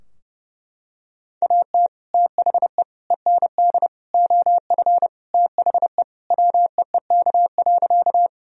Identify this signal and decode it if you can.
Morse code, heard as a keyed tone, about 30 wpm, 700 Hz. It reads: AT THE END OF THE WEEK.
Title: Sentences from Top 1000 Words - Review Part 2 - 30wpm